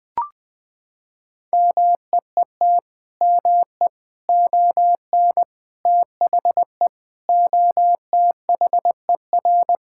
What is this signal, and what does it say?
Morse code, heard as a keyed tone, about 21 wpm, 700 Hz. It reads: MEET ME ON THE OTHER
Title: Sentences with 6-Word Phrases - 20wpm